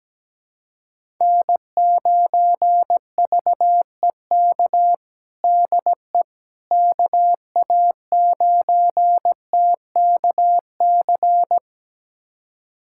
Morse code, heard as a keyed tone, about 17 wpm, 700 Hz. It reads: N9VEK DE KA9TKC
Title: USA Callsign DE Callsign 17wpm